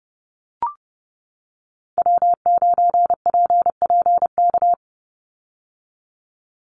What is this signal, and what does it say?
Morse code, heard as a keyed tone, about 30 wpm, 700 Hz. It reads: W9PPK
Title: USA Call Signs 30wpm